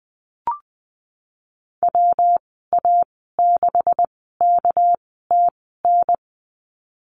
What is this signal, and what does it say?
Morse code, heard as a keyed tone, about 20 wpm, spaced as for 14 wpm, 700 Hz. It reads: WA6KTN
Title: 200 Lesson 32g - Letter k - Call Signs 20wpm